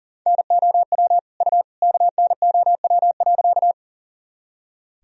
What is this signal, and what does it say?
Morse code, heard as a keyed tone, about 40 wpm, 700 Hz. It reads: NOW U KNOW.
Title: Sentences from Top 100 Words and Abbreviations 40wpm